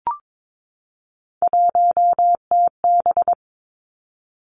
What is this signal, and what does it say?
Morse code, heard as a keyed tone, about 22 wpm, 700 Hz. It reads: 1TB
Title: Three Challenging Letter-Number Combinations 22wpm